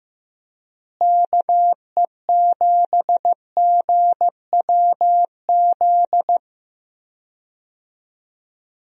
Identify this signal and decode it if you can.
Morse code, heard as a keyed tone, about 15 wpm, 700 Hz. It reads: KE7GWZ
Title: USA Call Signs - 6 Characters - Repeated 3x - 15wpm